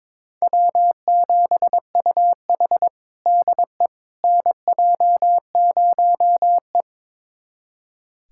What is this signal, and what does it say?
Morse code, heard as a keyed tone, about 21 wpm, 700 Hz. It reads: W7UH DE NJ0E
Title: USA Callsign DE Callsign - 4 Characters - 22wpm